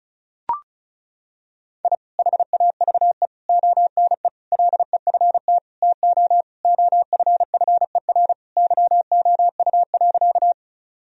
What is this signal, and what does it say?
Morse code, heard as a keyed tone, about 36 wpm, 700 Hz. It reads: I HAVE ONE LEFT TO OFFER YOU.
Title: Sentences from Top 1000 Words - Review Part 4 - 35wpm